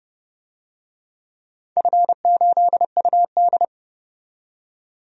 Morse code, heard as a keyed tone, about 30 wpm, 700 Hz. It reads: F8UD
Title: DXCC-100 Call Signs - 4 Characters - 30wpm